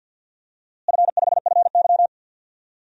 Morse code, heard as a keyed tone, about 46 wpm, 700 Hz. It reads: F5FY
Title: DXCC-100 Call Signs - 4 Characters - Repeated 2x - 50wpm